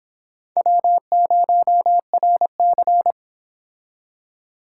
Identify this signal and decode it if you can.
Morse code, heard as a keyed tone, about 25 wpm, 700 Hz. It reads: W0RC